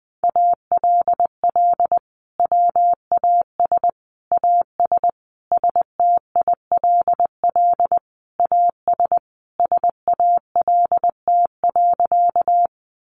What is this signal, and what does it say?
Morse code, heard as a keyed tone, about 20 wpm, 700 Hz. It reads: ALL WAS AS STILL AS SALT.
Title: Sentences from Top 800 Words 20wpm